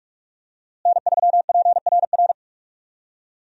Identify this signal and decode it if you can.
Morse code, heard as a keyed tone, about 44 wpm, 700 Hz. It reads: N3PRR
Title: USA Call Signs - 5 Characters - Repeated 3x - 45wpm